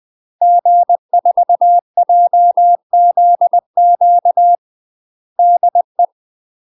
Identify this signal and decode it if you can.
Morse code, heard as a keyed tone, about 20 wpm, 700 Hz. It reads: G4JZQ DE
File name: UK MASTER QSOs TRAINING / 20WPM / UK MASTER QSOs 0030